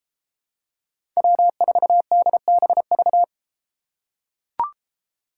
Morse code, heard as a keyed tone, about 33 wpm, 700 Hz. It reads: W4DBV